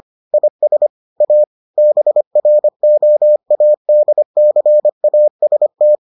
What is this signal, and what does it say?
Morse code, heard as a keyed tone, about 25 wpm, 600 Hz. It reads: IS A BROADCAST